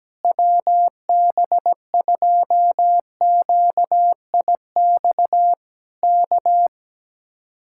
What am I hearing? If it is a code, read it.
Morse code, heard as a keyed tone, about 17 wpm, 700 Hz. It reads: WB2QIX K
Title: Speed-Racing Common QSO Phrases - 17wpm